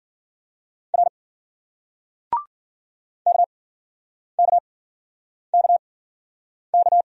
Morse code, heard as a keyed tone, about 52 wpm, spaced as for 9 wpm, 700 Hz. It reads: R KKKK